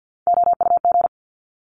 Morse code, heard as a keyed tone, about 50 wpm, 700 Hz. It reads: OUG
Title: Top Three Letter Combinations in Words 50wpm